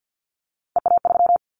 Morse code, heard as a keyed tone, about 50 wpm, 700 Hz. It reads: EA3
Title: Callsign Prefixes for Top 25 DXCC - 50wpm